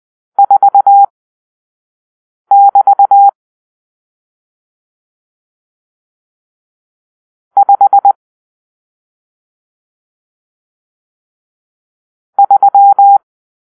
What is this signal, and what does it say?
Morse code, heard as a keyed tone, about 20 wpm, spaced as for 5 wpm, 800 Hz. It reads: 4<BT> 5 3
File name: arrl / Grp-6 Drill-1